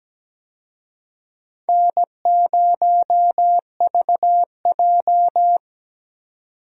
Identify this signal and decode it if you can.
Morse code, heard as a keyed tone, about 17 wpm, 700 Hz. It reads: N0VJ
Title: USA Call Signs - 4 Characters - 17wpm